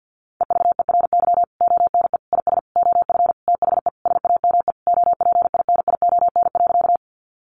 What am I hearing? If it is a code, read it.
Morse code, heard as a keyed tone, about 50 wpm, 700 Hz. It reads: EVERY ONE IS OF THE SAME OPINION.
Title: Sentences from Top 1200 Words 50wpm